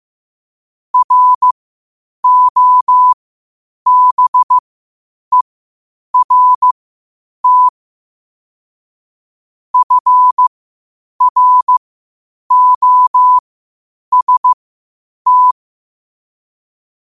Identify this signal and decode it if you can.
Morse code, heard as a keyed tone, about 15 wpm, spaced as for 8 wpm, 1000 Hz. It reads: ROBERT FROST
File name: qotdfiles / 8wpm